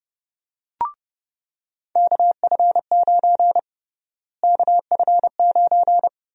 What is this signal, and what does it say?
Morse code, heard as a keyed tone, about 30 wpm, 700 Hz. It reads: KF9 KF9
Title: USA Call Sign Prefixes - Repeated 2x - 30wpm